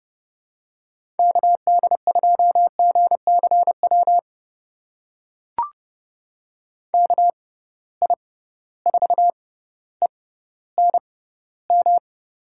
Morse code, heard as a keyed tone, about 30 wpm, spaced as for 15 wpm, 700 Hz. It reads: KD2GCW KI4ENM
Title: USA Call Signs - 6 Characters - 6x Character Spacing - 30wpm